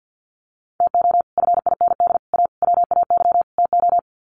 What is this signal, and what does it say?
Morse code, heard as a keyed tone, about 48 wpm, 700 Hz. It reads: TO FIND A WAY TO
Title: Sets of 5 Words 50wpm